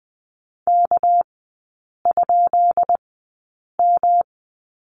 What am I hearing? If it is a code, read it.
Morse code, heard as a keyed tone, about 20 wpm, spaced as for 8 wpm, 700 Hz. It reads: K?M
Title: 198 Lesson 32e - Letter k - Three Characters 20wpm